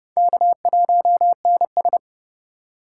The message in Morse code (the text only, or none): K1NS